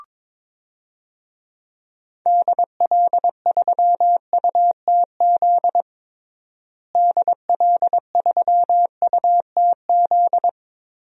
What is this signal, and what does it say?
Morse code, heard as a keyed tone, about 22 wpm, 700 Hz. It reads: DL3UTZ DL3UTZ
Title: DXCC-100 Call Signs - 6 Characters - Repeated 3x - 22wpm